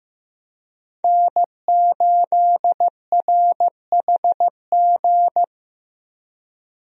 Morse code, heard as a keyed tone, about 15 wpm, 700 Hz. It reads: N8RHG
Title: USA Call Signs - 5 Characters - 15wpm